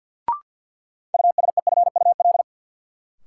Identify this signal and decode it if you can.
Morse code, heard as a keyed tone, about 51 wpm, 700 Hz. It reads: USEFUL
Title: Sentences from Top 700 Words - New Word Drill - 50wpm